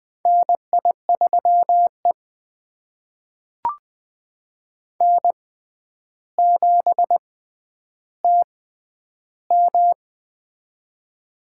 Morse code, heard as a keyed tone, about 20 wpm, spaced as for 10 wpm, 700 Hz. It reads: NI3E N7TM
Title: USA Call Signs - 4 Characters - 6x Character Spacing - 20wpm